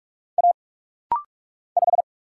Morse code, heard as a keyed tone, about 44 wpm, 700 Hz. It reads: A 5
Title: Single Letter-Number - Rapid-Fire 45wpm